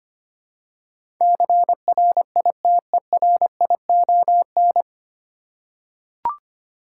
Morse code, heard as a keyed tone, about 25 wpm, 700 Hz. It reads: CRITERION
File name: Nine Letter Words - 25wpm